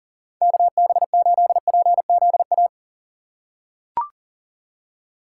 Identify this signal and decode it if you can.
Morse code, heard as a keyed tone, about 40 wpm, 700 Hz. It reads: KB8PZA
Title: USA Call Signs - 6 Characters - 2x Character Spacing - 40wpm